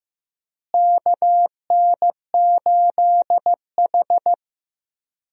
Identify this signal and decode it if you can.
Morse code, heard as a keyed tone, about 15 wpm, 700 Hz. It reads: KN8H
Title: USA Call Signs - 4 Characters - Repeated 2x - 15wpm